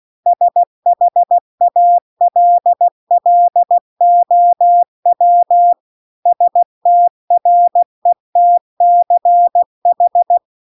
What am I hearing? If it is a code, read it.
Morse code, heard as a keyed tone, about 16 wpm, 700 Hz. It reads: SHALLOW STRETCH